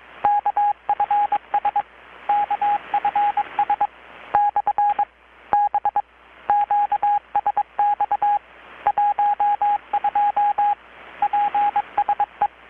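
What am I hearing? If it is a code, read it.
Morse code, heard as a keyed tone, about 22 wpm, 820 Hz, with radio noise. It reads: KFS KFS / B QSX 12 PSE
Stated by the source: Morse from KPH's Night of Nights XX maritime radiotelegraphy event, received in Chiapas, Mexico about 3,800 km away on a Kenwood R-600 with a grounded 10 m long-wire antenna.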